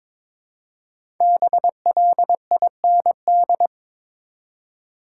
Morse code, heard as a keyed tone, about 21 wpm, 700 Hz. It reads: BLIND
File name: 5 Letter Words - First 3 Letters Given - 22wpm